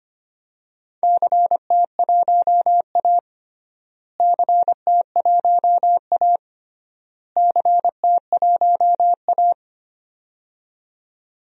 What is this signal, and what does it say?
Morse code, heard as a keyed tone, about 25 wpm, 700 Hz. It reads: CT1A CT1A CT1A
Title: DXCC-100 Call Signs - 4 Characters - Repeated 3x - 25wpm